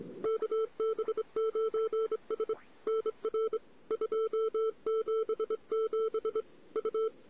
Morse code, heard as a keyed tone, about 25 wpm, 1320 Hz, with noise. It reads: KB9S NR 277 U
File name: CW / 2008-ss-cw-k5tr-op-n5rz -14:00